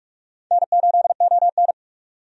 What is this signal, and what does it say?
Morse code, heard as a keyed tone, about 46 wpm, 700 Hz. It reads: N8ON
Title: USA Call Signs - 4 Characters - 45wpm